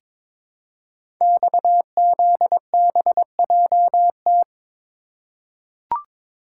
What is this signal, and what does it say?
Morse code, heard as a keyed tone, about 22 wpm, 700 Hz. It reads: XZBJT
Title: Code Groups - 5 Letter:Numbers - 22wpm